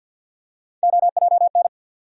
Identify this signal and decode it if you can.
Morse code, heard as a keyed tone, about 47 wpm, 700 Hz. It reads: OJN
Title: USA Call Sign Postfixes - 3 Letters 50wpm